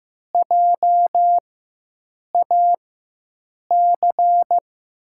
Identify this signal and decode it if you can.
Morse code, heard as a keyed tone, about 15 wpm, spaced as for 7 wpm, 700 Hz. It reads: JAC
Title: USA Call Signs - 6 Characters - 4x Character Spacing - 15wpm